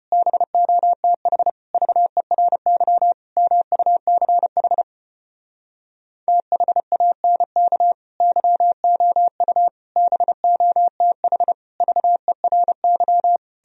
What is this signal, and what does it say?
Morse code, heard as a keyed tone, about 33 wpm, 700 Hz. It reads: BOTH VERY MUCH THANK YOU BOTH VERY